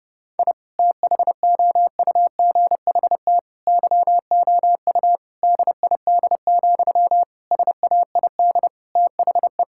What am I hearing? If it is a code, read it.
Morse code, heard as a keyed tone, about 30 wpm, 700 Hz. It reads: I THOUGHT YOU DID, SAID THE
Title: Sentences from Top 1000 Words - Review Part 5 - 30wpm